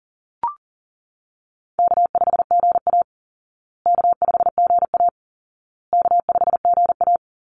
Morse code, heard as a keyed tone, about 40 wpm, 700 Hz. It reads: K5GA K5GA K5GA
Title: Easy Call Signs Repeated 3x - 40wpm